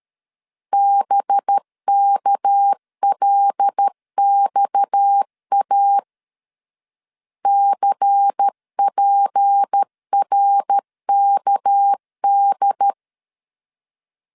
Morse code, heard as a keyed tone, about 13 wpm, 800 Hz. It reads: BKLXA CPRKD
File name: LEZIONI / 6 PKX0 / 6 PKX 12wpm 2024